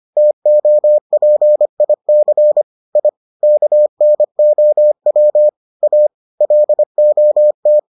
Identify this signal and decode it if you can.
Morse code, heard as a keyed tone, about 25 wpm, 600 Hz. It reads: TOPIC I KNOW A LOT